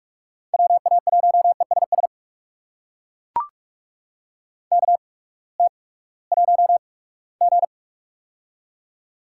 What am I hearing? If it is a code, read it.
Morse code, heard as a keyed tone, about 42 wpm, spaced as for 22 wpm, 700 Hz. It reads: WA1ESS KT1G
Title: USA Call Signs - 8x Character Spacing - 45wpm